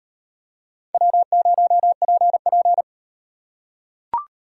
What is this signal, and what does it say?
Morse code, heard as a keyed tone, about 38 wpm, 700 Hz. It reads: W0PP